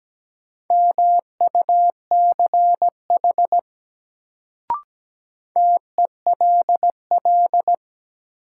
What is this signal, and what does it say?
Morse code, heard as a keyed tone, about 17 wpm, 700 Hz. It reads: MUCH TELL
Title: Sentences from Top 200 Words - New Word Drill - 17wpm